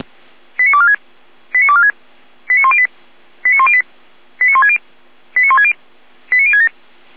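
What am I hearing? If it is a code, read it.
Five-tone selective call (ZVEI), heard as quick runs of five tones, tones of about 70 ms. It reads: 89246 89246 89108 89108 89160 89160 89067
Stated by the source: ZVEI five-tone call received by SDR.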